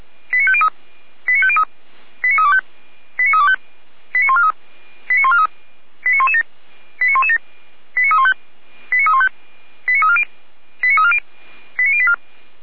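Five-tone selective call (ZVEI), heard as quick runs of five tones, tones of about 70 ms. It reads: 89503 89503 89325 89325 89143 89143 89117 89117 89315 89340 89340 89084
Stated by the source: ZVEI five-tone selective call received by SDR, from the feuerwehr-orlamuende.org fire brigade server.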